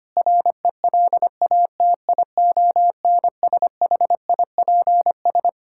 Morse code, heard as a keyed tone, about 25 wpm, 700 Hz. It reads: RELATIONSHIPS
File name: practice / 25wpm / All-Age Appropriate Jokes - 25wpm